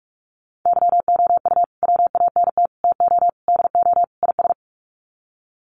Morse code, heard as a keyed tone, about 44 wpm, 690 Hz. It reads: YOU WANT TO DO IS
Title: Sets of 5 Words Encore2 45wpm